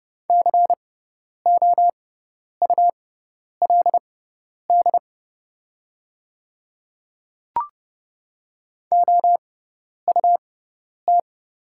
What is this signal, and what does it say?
Morse code, heard as a keyed tone, about 30 wpm, spaced as for 10 wpm, 700 Hz. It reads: COULD OUT
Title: 6x Character Spacing - Top 200 Words - 30wpm